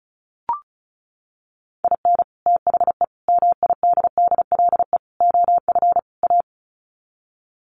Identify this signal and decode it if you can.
Morse code, heard as a keyed tone, about 34 wpm, 700 Hz. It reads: IN THE MIDDLE OF A
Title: Sets of 5 Words 35wpm